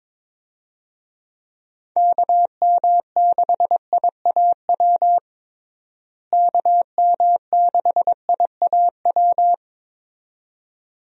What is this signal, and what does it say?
Morse code, heard as a keyed tone, about 22 wpm, 700 Hz. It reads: KM6IAW KM6IAW
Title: USA Call Signs - 6 Characters - Repeated 2x - 22wpm